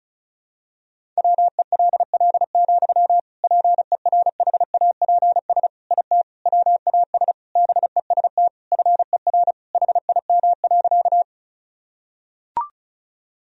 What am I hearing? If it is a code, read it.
Morse code, heard as a keyed tone, about 36 wpm, 700 Hz. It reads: WELL, PERHAPS IT WAS BEST FER HIM.
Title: Sentences from Top 500 Words and Abbreviations 35wpm